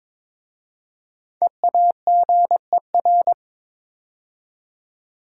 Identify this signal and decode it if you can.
Morse code, heard as a keyed tone, about 22 wpm, 700 Hz. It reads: EAGER
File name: 5 Letter Words - First 3 Letters Given - 22wpm